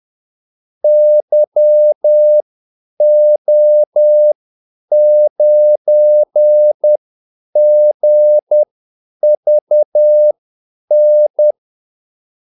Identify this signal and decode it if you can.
Morse code, heard as a keyed tone, about 10 wpm, spaced as for 8 wpm, 600 Hz. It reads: YO9GVN
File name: Callsigns-10-8wpm